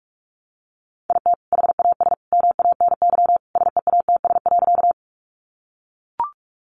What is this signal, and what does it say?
Morse code, heard as a keyed tone, about 44 wpm, 700 Hz. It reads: IT HAS MANY SEATS.
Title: Additional Easy Phrases 45wpm